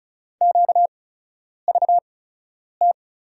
Morse code, heard as a keyed tone, about 36 wpm, spaced as for 10 wpm, 700 Hz. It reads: QVT
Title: USA Call Signs - 6 Characters - 8x Character Spacing - 35wpm